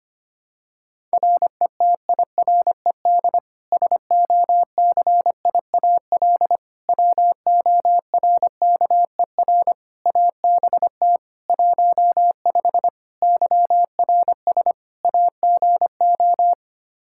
Morse code, heard as a keyed tone, about 25 wpm, 700 Hz. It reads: RETIRED SOCIAL WORKER ABT 15 YRS AGO
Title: Practice QSO - Exchange 3 - More Info - 25wpm